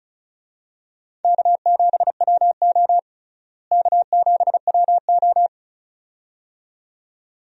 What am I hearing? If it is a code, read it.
Morse code, heard as a keyed tone, about 36 wpm, 700 Hz. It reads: K7WO K7WO
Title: USA Call Signs - 4 Characters - Repeated 2x - 35wpm